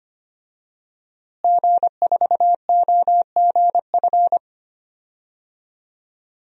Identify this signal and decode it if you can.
Morse code, heard as a keyed tone, about 25 wpm, 700 Hz. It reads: G4OGF
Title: DXCC-100 Call Signs - 5 Characters - Repeated 2x - 25wpm